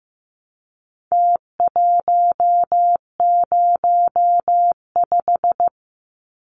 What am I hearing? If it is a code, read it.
Morse code, heard as a keyed tone, about 15 wpm, 700 Hz. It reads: T105
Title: Numbers - Contest Style - 4-Digit 15wpm